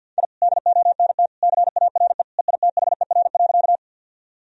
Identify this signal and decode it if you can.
Morse code, heard as a keyed tone, about 50 wpm, 700 Hz. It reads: I DONT CARE EITHER.
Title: Sentences from Top 1000 Words - Review Part 4 - 50wpm